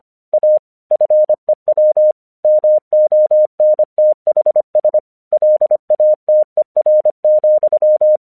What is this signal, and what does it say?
Morse code, heard as a keyed tone, about 25 wpm, 610 Hz. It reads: A FEW MONTHS LATER,
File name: 1 English 25wpm / AQC06